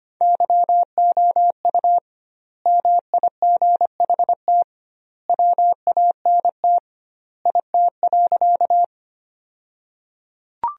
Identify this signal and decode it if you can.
Morse code, heard as a keyed tone, about 25 wpm, 700 Hz. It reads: YOU MIGHT WANT IT.
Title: 2x Word-Spacing - Sentences from Top 300 Words - 25wpm